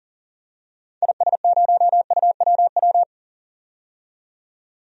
Morse code, heard as a keyed tone, about 38 wpm, 700 Hz. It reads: IS0UWW
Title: DXCC-100 Call Signs - 6 Characters - 40wpm